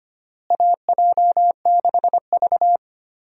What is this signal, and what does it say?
Morse code, heard as a keyed tone, about 25 wpm, 700 Hz. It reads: AJ6V
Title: USA Call Signs - 4 Characters - Repeated 3x - 25wpm